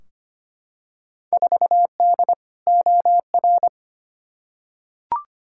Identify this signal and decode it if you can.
Morse code, heard as a keyed tone, about 25 wpm, 700 Hz. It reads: 4D OR
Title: Field Day Exchanges Encore - 25wpm